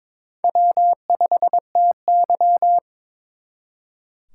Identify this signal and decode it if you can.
Morse code, heard as a keyed tone, about 26 wpm, 700 Hz. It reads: W5TY
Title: Speed Racing - USA 1x2 Call Signs 22wpm